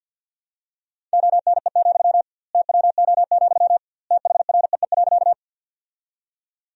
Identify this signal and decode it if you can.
Morse code, heard as a keyed tone, about 49 wpm, 700 Hz. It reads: ONE, TWO, THREE.
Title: 2x Word-Spacing - Sentences from Top 200 Words - 50wpm